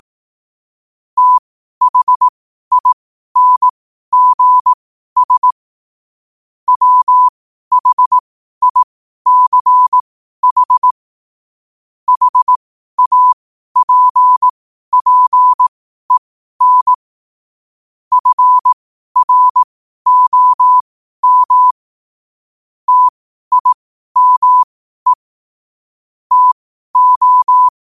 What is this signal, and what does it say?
Morse code, heard as a keyed tone, about 18 wpm, spaced as for 12 wpm, 1000 Hz. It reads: THINGS WHICH HAPPEN FROM TIME TO
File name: qotdfiles / 12wpm